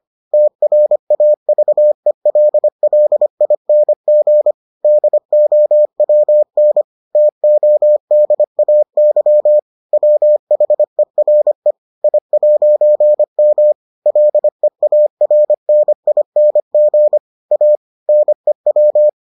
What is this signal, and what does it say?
Morse code, heard as a keyed tone, about 25 wpm, 600 Hz. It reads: TRAVELLING DOWN TODAY WHERE I'M LEARNING A NEW